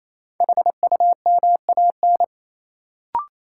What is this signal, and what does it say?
Morse code, heard as a keyed tone, about 28 wpm, 700 Hz. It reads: HUMAN
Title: Sentences from Top 600 Words - New Word Drill - 28wpm